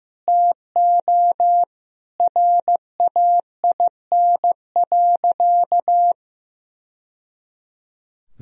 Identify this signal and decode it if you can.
Morse code, heard as a keyed tone, about 16 wpm, 700 Hz. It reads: TO RAIN.
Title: Sentences from Top 1000 Words - Review Part 1 - 15wpm